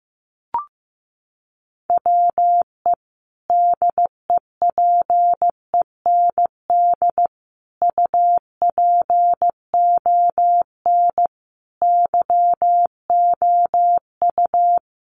Morse code, heard as a keyed tone, about 15 wpm, 700 Hz. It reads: WE DEPEND UPON YOU
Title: Sentences from Top 1300 Words 15wpm